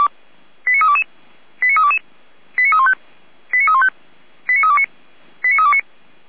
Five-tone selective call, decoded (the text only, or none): 89322 89322 89315 89315 89329 89329